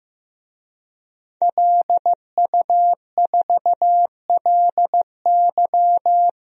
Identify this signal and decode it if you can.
Morse code, heard as a keyed tone, about 16 wpm, 700 Hz. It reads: LU4LY
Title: DXCC-100 Call Signs - 5 Characters - Repeated 3x - 15wpm